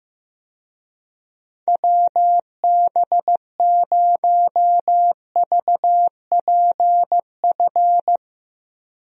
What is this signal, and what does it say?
Morse code, heard as a keyed tone, about 16 wpm, 700 Hz. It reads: WB0VPF
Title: USA Call Signs - 6 Characters - Repeated 3x - 15wpm